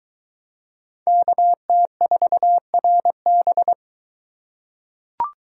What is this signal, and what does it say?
Morse code, heard as a keyed tone, about 20 wpm, 700 Hz. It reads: KT4RB